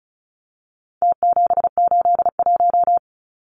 Morse code, heard as a keyed tone, about 34 wpm, 700 Hz. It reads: T781
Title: Numbers - Contest Style - 4-Digit 35wpm